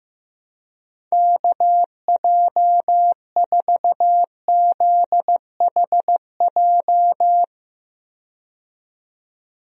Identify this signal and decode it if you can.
Morse code, heard as a keyed tone, about 15 wpm, 700 Hz. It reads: KJ4ZHJ
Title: USA Call Signs - 6 Characters - 15wpm